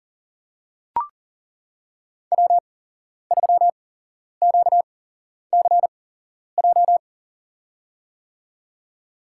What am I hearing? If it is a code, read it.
Morse code, heard as a keyed tone, about 40 wpm, spaced as for 11 wpm, 700 Hz. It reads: W3QCJ
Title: USA Call Signs - 5 Characters - 8x Character Spacing - 40wpm